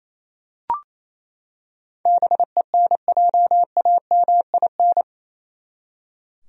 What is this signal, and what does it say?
Morse code, heard as a keyed tone, about 28 wpm, 700 Hz. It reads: BENJAMIN
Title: USA Common Names - 28wpm